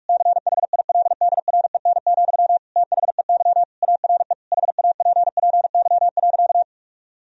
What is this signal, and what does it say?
Morse code, heard as a keyed tone, about 46 wpm, 700 Hz. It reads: CHILDREN, THEY ARE HAPPY.
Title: Sentences from Top 1000 Words - Review Part 3 - 45wpm